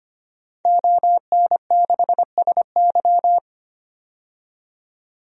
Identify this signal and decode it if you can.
Morse code, heard as a keyed tone, about 25 wpm, 700 Hz. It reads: ON6SY
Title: DXCC-100 Call Signs - 5 Characters - 25wpm